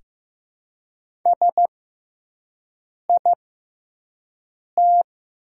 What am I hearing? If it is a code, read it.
Morse code, heard as a keyed tone, about 15 wpm, spaced as for 5 wpm, 690 Hz. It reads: SIT
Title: 6x Character Spacing - Top 500 Words - 15wpm